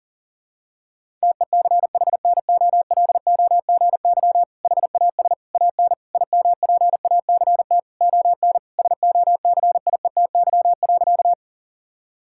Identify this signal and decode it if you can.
Morse code, heard as a keyed tone, about 40 wpm, 700 Hz. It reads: TECHNOLOGY HAS AN IMPACT ON SOCIETY.
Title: Sentences from Top 1000 Words - Review Part 6 - 40wpm